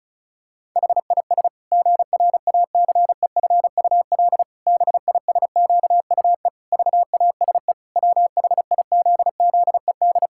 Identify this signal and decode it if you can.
Morse code, heard as a keyed tone, about 36 wpm, 700 Hz. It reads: HIS GRACEFUL BISQUE VASE WHIZZED
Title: Pangram Challenge v2 - 35wpm